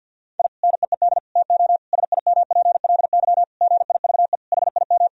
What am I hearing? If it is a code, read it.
Morse code, heard as a keyed tone, about 50 wpm, 700 Hz. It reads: I NEED TO SIMPLY GIVE HIM
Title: Sentences from Top 1000 Words - Review Part 4 - 50wpm